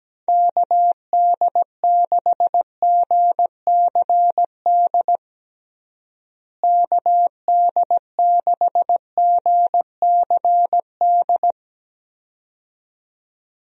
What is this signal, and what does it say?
Morse code, heard as a keyed tone, about 17 wpm, 700 Hz. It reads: KD6GCD KD6GCD
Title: USA Call Signs - 6 Characters - Repeated 2x - 17wpm